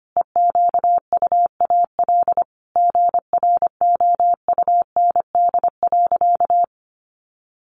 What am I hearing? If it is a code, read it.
Morse code, heard as a keyed tone, about 25 wpm, 700 Hz. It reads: EQUAL GROUND.
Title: Sentences from Top 1400 Words 25wpm